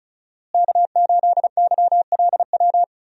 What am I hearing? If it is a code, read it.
Morse code, heard as a keyed tone, about 36 wpm, 700 Hz. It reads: K8YLW
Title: USA Call Signs - 5 Characters - Repeated 3x - 35wpm